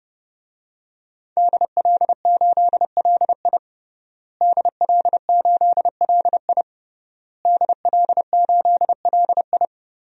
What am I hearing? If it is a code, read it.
Morse code, heard as a keyed tone, about 30 wpm, 700 Hz. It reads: DL8LI DL8LI DL8LI
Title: DXCC-100 Call Signs - 5 Characters - Repeated 3x - 30wpm